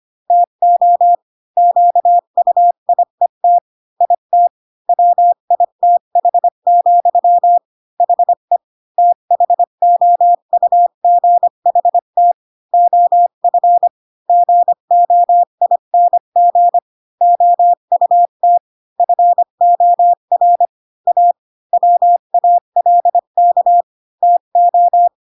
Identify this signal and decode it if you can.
Morse code, heard as a keyed tone, about 25 wpm, 700 Hz. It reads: TO QUIET IT WITH, HE THOUGHT OF GOING OUT FOR A WALK TO
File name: Pinocchio / Pinocchio ch5 20 25 ik1hge